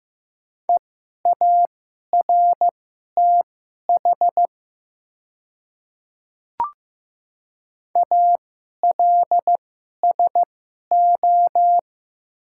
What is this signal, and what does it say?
Morse code, heard as a keyed tone, about 15 wpm, spaced as for 11 wpm, 700 Hz. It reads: EARTH ALSO
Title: 2x Character Spacing - Top 300 Words - 15wpm